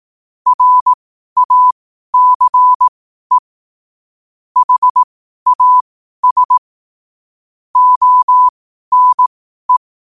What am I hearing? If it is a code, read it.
Morse code, heard as a keyed tone, about 18 wpm, spaced as for 12 wpm, 1000 Hz. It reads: RACE HAS ONE
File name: qotdfiles / 12wpm